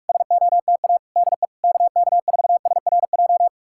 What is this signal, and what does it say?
Morse code, heard as a keyed tone, about 69 wpm, spaced as for 56 wpm, 700 Hz. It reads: SOTA DE KK4SRJ
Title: Speed-Racing Common QSO Phrases Encore - 45wpm